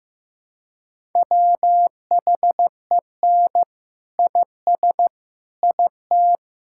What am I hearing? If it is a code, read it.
Morse code, heard as a keyed tone, about 15 wpm, 700 Hz. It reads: WHEN IS IT
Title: Sentences from Top 1000 Words - Review Part 6 - 15wpm